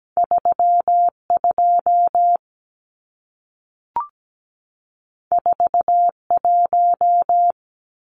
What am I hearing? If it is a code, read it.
Morse code, heard as a keyed tone, about 17 wpm, 700 Hz. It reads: 32 41
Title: Numbers - 2-Digit 17wpm